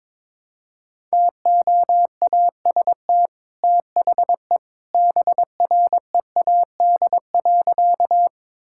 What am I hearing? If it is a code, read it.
Morse code, heard as a keyed tone, about 22 wpm, 700 Hz. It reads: TOAST THE BREAD.
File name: practice / 22wpm / Easy Sentences with Hints - 22wpm